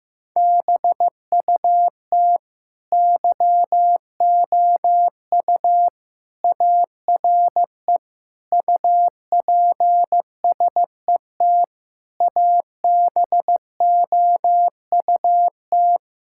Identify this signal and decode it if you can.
Morse code, heard as a keyed tone, about 15 wpm, 700 Hz. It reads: BUT YOU ARE UPSET ABOUT
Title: Sentences from Top 1000 Words - Review Part 5 - 15wpm